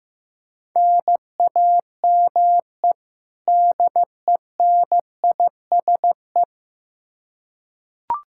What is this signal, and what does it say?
Morse code, heard as a keyed tone, about 15 wpm, 700 Hz. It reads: NAME DENISE
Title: Practice QSO - Exchange 1 - Beginning - 15wpm